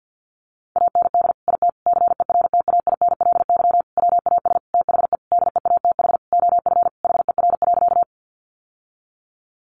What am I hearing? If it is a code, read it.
Morse code, heard as a keyed tone, about 50 wpm, 700 Hz. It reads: AND IT CERTAINLY WAS THE DEATH OF HER.
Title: Sentences from Top 1100 Words 50wpm